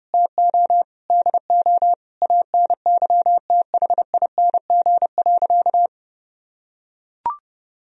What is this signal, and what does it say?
Morse code, heard as a keyed tone, about 30 wpm, 700 Hz. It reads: TO DO ANYTHING.
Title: Sentences from Top 1000 Words - Review Part 2 - 30wpm